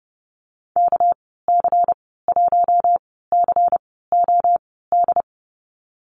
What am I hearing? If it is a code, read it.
Morse code, heard as a keyed tone, about 30 wpm, spaced as for 17 wpm, 700 Hz. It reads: KC1COD
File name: Lesson 37g - Letter x - Call Signs 30wpm